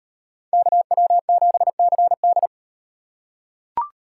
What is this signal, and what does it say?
Morse code, heard as a keyed tone, about 38 wpm, 700 Hz. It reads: KW7CD